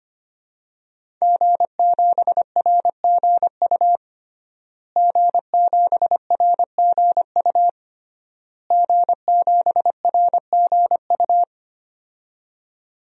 Morse code, heard as a keyed tone, about 25 wpm, 700 Hz. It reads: G7RGU G7RGU G7RGU
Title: DXCC-100 Call Signs - 5 Characters - Repeated 3x - 25wpm